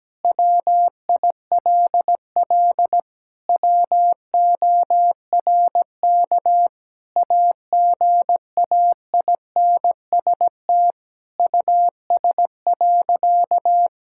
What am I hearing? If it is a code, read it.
Morse code, heard as a keyed tone, about 17 wpm, 700 Hz. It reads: WILL WORK AGAINST US.
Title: Sentences from Top 1000 Words - Review Part 6 - 17wpm